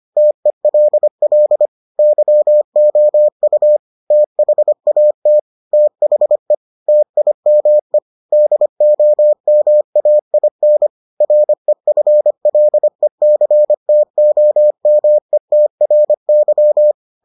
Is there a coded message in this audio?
Yes: Morse code from 0.2 to 16.9 s, heard as a keyed tone, about 25 wpm, 600 Hz. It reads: TELL YOU THAT THE TIME DOMAIN REFLECTOMETRY